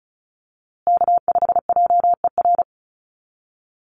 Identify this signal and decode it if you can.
Morse code, heard as a keyed tone, about 34 wpm, 700 Hz. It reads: K5JER
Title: USA Call Signs Repeated 3x 35wpm